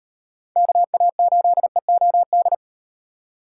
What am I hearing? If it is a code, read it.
Morse code, heard as a keyed tone, about 38 wpm, 700 Hz. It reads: KA8EOD